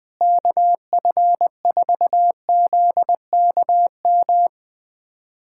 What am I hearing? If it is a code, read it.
Morse code, heard as a keyed tone, about 21 wpm, 700 Hz. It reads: KF4ZKM
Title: USA Callsign DE Callsign 20wpm